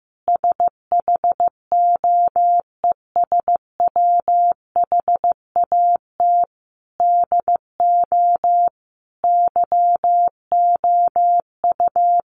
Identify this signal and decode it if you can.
Morse code, heard as a keyed tone, about 15 wpm, 700 Hz. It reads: SHOES WHAT DO YOU
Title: A Sound Of Thunder 15wpm